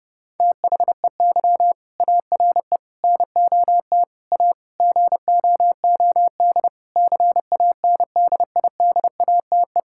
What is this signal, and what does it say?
Morse code, heard as a keyed tone, about 30 wpm, 700 Hz. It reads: THEY ARE NOT A GOOD CANDIDATE
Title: Sentences from Top 1600 Words 30wpm